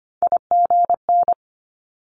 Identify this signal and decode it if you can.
Morse code, heard as a keyed tone, about 25 wpm, 700 Hz. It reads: IGN
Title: Multi-letter Phonemes 25wpm